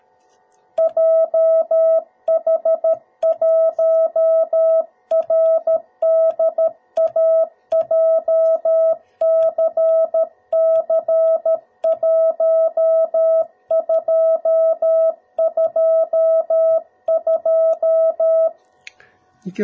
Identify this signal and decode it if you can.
Morse code, heard as a keyed tone, about 13 wpm, 650 Hz. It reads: JH1RDA JCC1222
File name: jh8cbh / 2021cw / cw02b